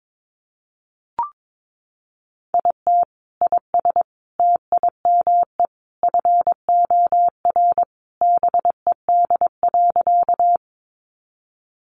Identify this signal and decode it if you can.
Morse code, heard as a keyed tone, about 22 wpm, 700 Hz. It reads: IT IS TIME FOR BED.
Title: Sentences from Top 500 Words 22wpm